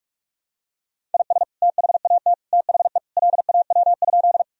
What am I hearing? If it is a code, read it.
Morse code, heard as a keyed tone, about 41 wpm, 700 Hz. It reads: IS THAT THE LAW?
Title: Sentences from Top 1000 Words - Review Part 3 - 45wpm